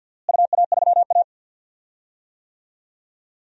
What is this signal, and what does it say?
Morse code, heard as a keyed tone, about 51 wpm, 700 Hz. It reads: UA3A
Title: DXCC-100 Call Signs - 4 Characters - Repeated 2x - 50wpm